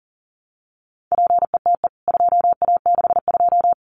Morse code, heard as a keyed tone, about 40 wpm, 700 Hz. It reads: PETE 2A62
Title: CWT Exchange 40wpm